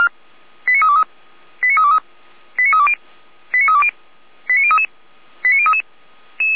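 Five-tone selective call (ZVEI), heard as quick runs of five tones, tones of about 80 ms. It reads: 89323 89323 89320 89320 89033 89033
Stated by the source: ZVEI five-tone selective call received on an SDR.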